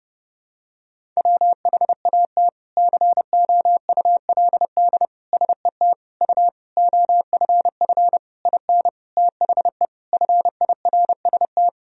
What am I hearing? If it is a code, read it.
Morse code, heard as a keyed tone, about 32 wpm, 700 Hz. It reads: WHAT COULD SET U OFF IN THE FIRST
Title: Sentences from Top 200 Words and Abbreviations 30wpm